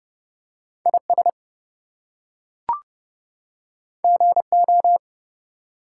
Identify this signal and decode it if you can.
Morse code, heard as a keyed tone, about 30 wpm, 700 Hz. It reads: IS GO
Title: Two Letter Words - 30wpm